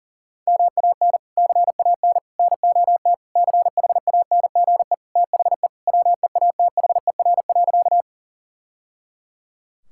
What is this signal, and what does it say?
Morse code, heard as a keyed tone, about 40 wpm, 700 Hz. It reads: MAN CAN NOT CHANGE THE WEATHER.
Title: Sentences from Top 1000 Words - Review Part 5 - 40wpm